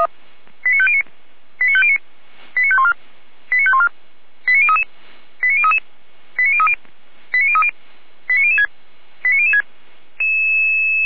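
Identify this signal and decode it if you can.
Five-tone selective call (ZVEI), heard as quick runs of five tones, tones of about 70 ms. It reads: 89509 89509 89514 89514 89033 89033 89030 89030 89006 89006
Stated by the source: ZVEI five-tone call received by SDR, fire brigade feuerwehr-orlamuende.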